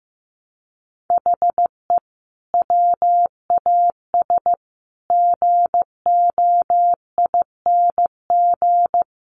Morse code, heard as a keyed tone, about 15 wpm, 700 Hz. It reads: HE WAS GOING
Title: Sets of 5 Words Encore2 15wpm